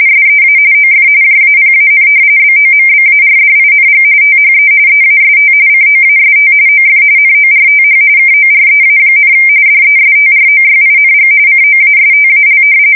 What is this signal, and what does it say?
RTTY, about 45.45 baud. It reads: SEA OF JAPAN BOMBARDING THE EAST COAST OF NORTH KOREA, DESTROYING COASTAL
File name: fsk170 / battleshipnj